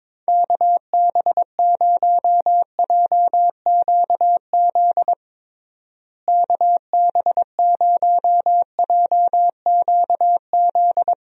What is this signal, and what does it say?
Morse code, heard as a keyed tone, about 21 wpm, 700 Hz. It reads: KB0JQZ KB0JQZ
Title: USA Call Signs - 6 Characters - Repeated 3x - 22wpm